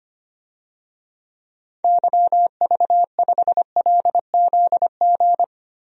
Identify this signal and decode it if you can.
Morse code, heard as a keyed tone, about 25 wpm, 700 Hz. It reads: YV5LZG
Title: DXCC-100 Call Signs - 6 Characters - Repeated 3x - 25wpm